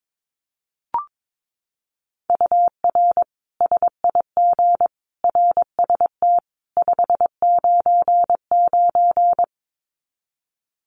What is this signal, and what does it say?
Morse code, heard as a keyed tone, about 22 wpm, 700 Hz. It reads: UR SIG RST 599
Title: Common QSO Phrases 22wpm